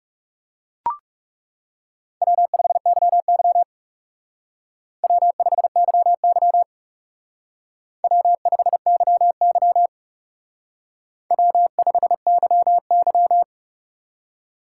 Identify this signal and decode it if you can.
Morse code, heard as a keyed tone, about 37 wpm, 700 Hz. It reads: W5YY W5YY W5YY W5YY
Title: Speed Racing - USA 1x2 Call Signs 30wpm